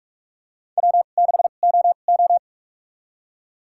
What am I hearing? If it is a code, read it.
Morse code, heard as a keyed tone, about 46 wpm, spaced as for 32 wpm, 700 Hz. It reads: W6OO
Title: USA Call Signs - 4 Characters - 2x Character Spacing - 45wpm